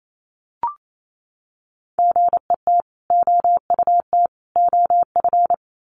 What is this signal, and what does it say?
Morse code, heard as a keyed tone, about 28 wpm, 700 Hz. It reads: GET OUT OF
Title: Sets of 3 Words 28wpm